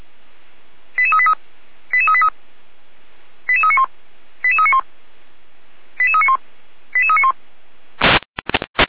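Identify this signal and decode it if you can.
Five-tone selective call (ZVEI), heard as quick runs of five tones, tones of about 70 ms. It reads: 88383 88383 88391 88391 88391 88391
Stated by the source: ZVEI five-tone selective call received on an SDR.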